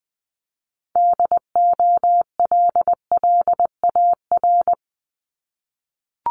Morse code, heard as a keyed tone, about 20 wpm, 700 Hz. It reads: DOLLAR
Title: Multi-letter Phonemes Repeated 3x 20wpm